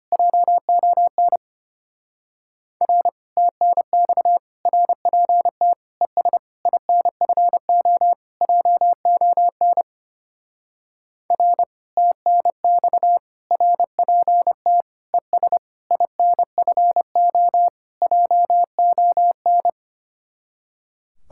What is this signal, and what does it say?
Morse code, heard as a keyed tone, about 28 wpm, 700 Hz. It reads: JON R TNX RPT ES INFO JON R TNX RPT ES INFO JON
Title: Speed-Racing Common QSO Phrases Encore - 25wpm